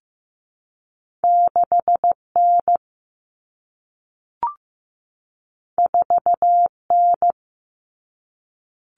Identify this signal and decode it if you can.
Morse code, heard as a keyed tone, about 15 wpm, 700 Hz. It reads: 6N 4N
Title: Common Cut Numbers - 2-Digit 15wpm